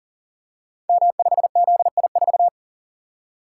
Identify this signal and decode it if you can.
Morse code, heard as a keyed tone, about 38 wpm, 690 Hz. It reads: M5ZI4
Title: Code Groups - 5 Letter:Numbers - 40wpm